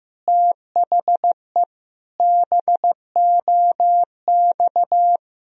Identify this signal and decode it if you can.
Morse code, heard as a keyed tone, about 15 wpm, 700 Hz. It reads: THE BOX
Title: Sentences from Top 1000 Words - Review Part 2 - 15wpm